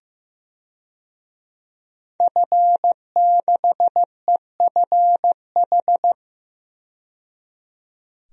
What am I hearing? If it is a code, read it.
Morse code, heard as a keyed tone, about 15 wpm, 700 Hz. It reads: F6EFH
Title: DXCC-100 Call Signs - 5 Characters - 15wpm